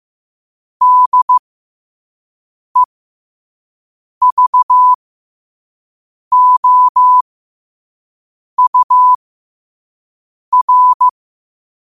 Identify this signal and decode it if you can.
Morse code, heard as a keyed tone, about 15 wpm, spaced as for 5 wpm, 1000 Hz. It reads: DEVOUR
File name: qotdfiles / 5wpm